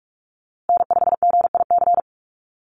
Morse code, heard as a keyed tone, about 44 wpm, 700 Hz. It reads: N5GIC
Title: USA Call Signs Repeated 2x 45wpm